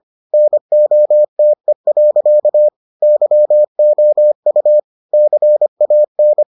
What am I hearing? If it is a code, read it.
Morse code, heard as a keyed tone, about 25 wpm, 600 Hz. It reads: NOTE. YOU CAN